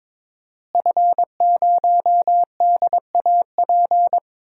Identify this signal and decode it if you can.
Morse code, heard as a keyed tone, about 21 wpm, 700 Hz. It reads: F0DAP
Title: DXCC-100 Call Signs - 5 Characters - Repeated 2x - 22wpm